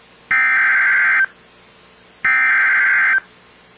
Emergency Alert System (SAME) tones (header bursts), heard as harsh buzzing bursts. It is heard from 0.3 to 3.1 s.